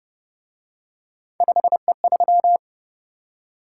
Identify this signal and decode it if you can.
Morse code, heard as a keyed tone, about 31 wpm, 700 Hz. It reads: 5E3
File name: Speed-Racing - Three Letter-Number Combinations - 25wpm